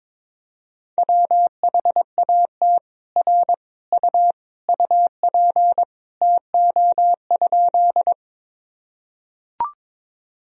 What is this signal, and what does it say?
Morse code, heard as a keyed tone, about 22 wpm, 700 Hz. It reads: WHAT R U UP TO?
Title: Sentences from Top 100 Words and Abbreviations 22wpm